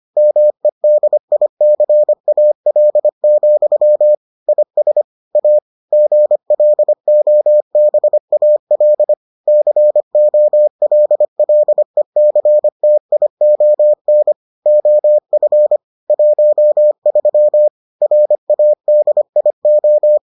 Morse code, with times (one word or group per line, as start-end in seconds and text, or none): 0.2-4.1 MEDICAL,
4.5-5.0 IS
5.4-5.6 A
5.9-9.1 GLOBAL
9.5-14.3 COLLECTION
14.7-15.8 OF
16.1-17.7 13
18.0-20.2 RADIO